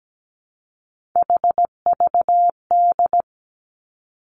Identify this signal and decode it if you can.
Morse code, heard as a keyed tone, about 17 wpm, 700 Hz. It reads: HVD
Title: Three Challenging Letter-Number Combinations 17wpm